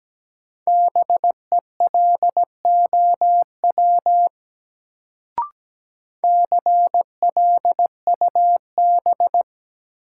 Morse code, heard as a keyed tone, about 17 wpm, 700 Hz. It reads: BELOW CLUB
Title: Sentences from Top 1200 Words - New Word Drill Alt - 17wpm